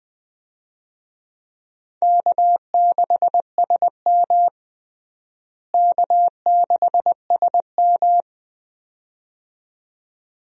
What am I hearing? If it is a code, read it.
Morse code, heard as a keyed tone, about 20 wpm, 700 Hz. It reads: K6SM K6SM
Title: USA Call Signs - 4 Characters - Repeated 2x - 20wpm